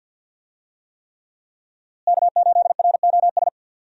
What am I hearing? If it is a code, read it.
Morse code, heard as a keyed tone, about 52 wpm, 700 Hz. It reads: K8ROS
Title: USA Call Signs - 5 Characters - Repeated 3x - 50wpm